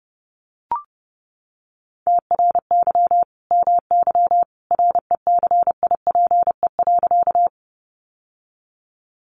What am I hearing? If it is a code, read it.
Morse code, heard as a keyed tone, about 30 wpm, 700 Hz. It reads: TRY MY RECIPE.
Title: Sentences from Top 1200 Words 30wpm